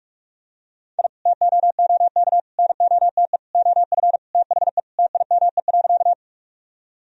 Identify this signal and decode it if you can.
Morse code, heard as a keyed tone, about 46 wpm, 700 Hz. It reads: I TOOK NOTE OF THE TIME.
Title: Sentences from Top 1000 Words - Review Part 2 - 45wpm